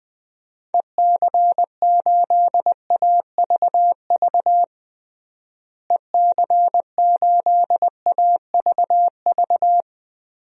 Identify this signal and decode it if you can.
Morse code, heard as a keyed tone, about 20 wpm, 700 Hz. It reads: EC8AVV EC8AVV
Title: DXCC-100 Call Signs - 6 Characters - Repeated 3x - 20wpm